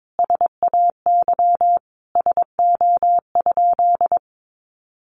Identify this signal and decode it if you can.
Morse code, heard as a keyed tone, about 22 wpm, 700 Hz. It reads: SAY SO?
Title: Sentences from Top 1400 Words 22wpm